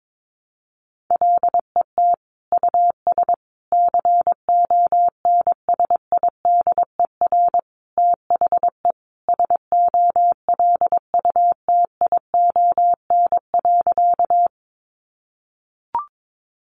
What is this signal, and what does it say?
Morse code, heard as a keyed tone, about 22 wpm, 700 Hz. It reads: LET US CONSIDER THE SOLUTION.
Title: Sentences from Top 1300 Words 22wpm